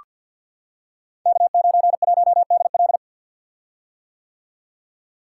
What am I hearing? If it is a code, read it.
Morse code, heard as a keyed tone, about 48 wpm, 700 Hz. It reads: K91DL
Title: Code Groups - 5 Letter:Numbers - 50wpm